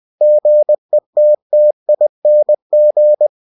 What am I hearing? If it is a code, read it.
Morse code, heard as a keyed tone, about 20 wpm, 600 Hz. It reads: GETTING
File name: cw-books / winnie-the-pooh / winnie-20wpm-page-0003